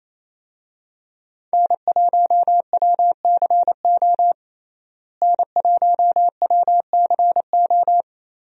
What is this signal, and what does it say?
Morse code, heard as a keyed tone, about 28 wpm, 700 Hz. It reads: N1WCO N1WCO
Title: USA Call Signs - 5 Characters - Repeated 3x - 28wpm